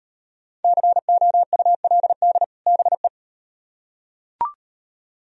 Morse code, heard as a keyed tone, about 38 wpm, 700 Hz. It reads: COULD BE